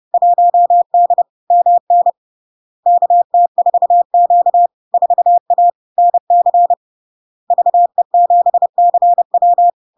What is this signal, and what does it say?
Morse code, heard as a keyed tone, about 30 wpm, 700 Hz. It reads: 1D MN KT4Q 4A NC VE7CW
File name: LARK-FD-2022-Log-in-CW-30wpm-700-Hz